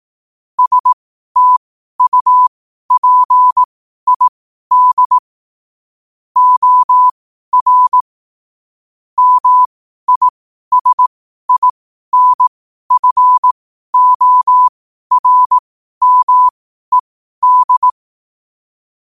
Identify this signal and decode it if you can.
Morse code, heard as a keyed tone, about 18 wpm, spaced as for 12 wpm, 1000 Hz. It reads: STUPID OR MISINFORMED
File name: qotdfiles / 12wpm